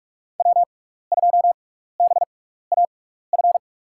Morse code, heard as a keyed tone, about 41 wpm, spaced as for 15 wpm, 700 Hz. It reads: W2BAF
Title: USA Call Signs - 5 Characters - 6x Character Spacing - 45wpm